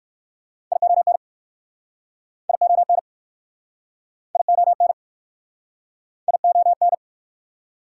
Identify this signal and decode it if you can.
Morse code, heard as a keyed tone, about 53 wpm, 700 Hz. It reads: ION ION ION ION
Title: Speed-Racing Prefixes and Suffixes 45wpm